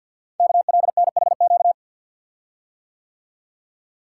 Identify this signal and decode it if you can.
Morse code, heard as a keyed tone, about 48 wpm, 700 Hz. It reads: KLNHQ
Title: Code Groups - 5 Letters - 50wpm